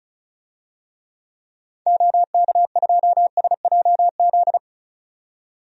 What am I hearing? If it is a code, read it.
Morse code, heard as a keyed tone, about 35 wpm, 700 Hz. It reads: OK2SJZ
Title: DXCC-100 Call Signs - 6 Characters - 35wpm